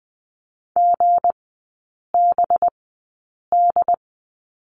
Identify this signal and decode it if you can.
Morse code, heard as a keyed tone, about 20 wpm, spaced as for 8 wpm, 690 Hz. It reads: GBD
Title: 205 Lesson 33e - Letter j - Three Characters 20wpm